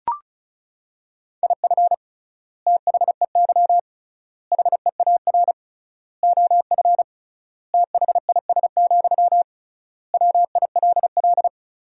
Morse code, heard as a keyed tone, about 36 wpm, 700 Hz. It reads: IF THEY HEAR OF THIS, WILL
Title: 3x Word-Spacing - Sentences from Top 400 Words - 35wpm